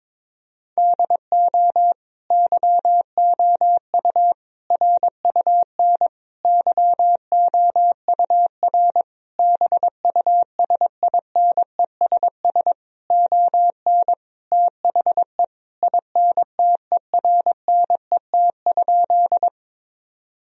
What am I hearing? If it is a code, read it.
Morse code, heard as a keyed tone, about 22 wpm, 700 Hz. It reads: DO YOU RUN YOUR BUSINESS ON THE INTERNET?
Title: Sentences from Top 1000 Words - Review Part 4 - 22wpm